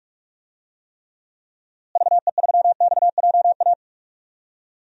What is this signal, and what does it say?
Morse code, heard as a keyed tone, about 45 wpm, 700 Hz. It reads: VE3XJA
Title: DXCC-100 Call Signs - 6 Characters - Repeated 2x - 45wpm